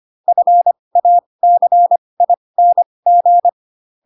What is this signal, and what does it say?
Morse code, heard as a keyed tone, about 25 wpm, spaced as for 20 wpm, 700 Hz. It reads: FACING
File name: Pinocchio / Pinocchio ch2 15 25 ik1hge